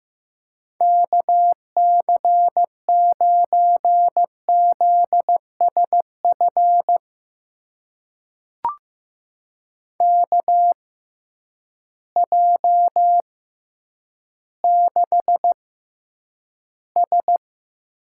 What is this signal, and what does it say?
Morse code, heard as a keyed tone, about 15 wpm, spaced as for 9 wpm, 700 Hz. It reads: KC9ZSF KJ6S
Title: USA Call Signs - 6 Characters - 6x Character Spacing - 15wpm